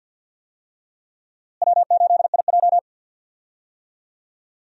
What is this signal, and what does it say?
Morse code, heard as a keyed tone, about 50 wpm, 700 Hz. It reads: W8IJ
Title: USA Call Signs - 4 Characters - Repeated 2x - 50wpm